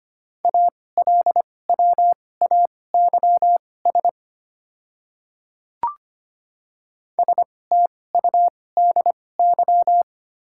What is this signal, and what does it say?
Morse code, heard as a keyed tone, about 25 wpm, spaced as for 18 wpm, 700 Hz. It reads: ALWAYS STUDY
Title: 2x Character Spacing - Top 300 Words - 25wpm